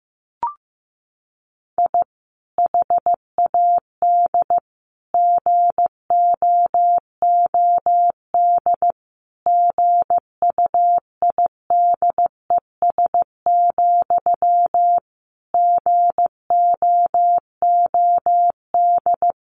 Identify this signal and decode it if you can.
Morse code, heard as a keyed tone, about 15 wpm, 700 Hz. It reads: I HAD GOOD GUIDES, GOOD
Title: A Sound Of Thunder 15wpm